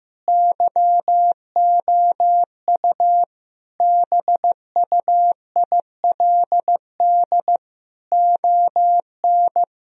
Morse code, heard as a keyed tone, about 15 wpm, 700 Hz. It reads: YOU BUILD ON
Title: Sentences from Top 1000 Words - Review Part 1 - 15wpm